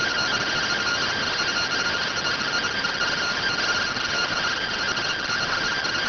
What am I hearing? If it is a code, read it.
Two-tone teleprinter signal, about 170 Hz shift.
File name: findings / SAGITAL fr15 nb203 3mm.gen-06 rx